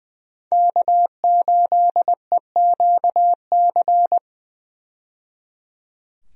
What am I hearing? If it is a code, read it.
Morse code, heard as a keyed tone, about 20 wpm, 700 Hz. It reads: K8EQC
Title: USA Call Signs - 5 Characters - 20wpm